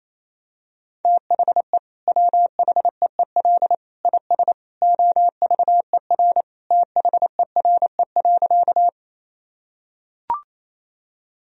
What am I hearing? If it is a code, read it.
Morse code, heard as a keyed tone, about 28 wpm, 700 Hz. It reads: THE WHEEL IS OVER THERE.
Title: Sentences from Top 1000 Words - Review Part 3 - 28wpm